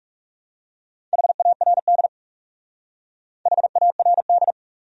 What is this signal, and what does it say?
Morse code, heard as a keyed tone, about 37 wpm, 700 Hz. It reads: HARD HARD
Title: Speed-Racing Top 300 Words 30wpm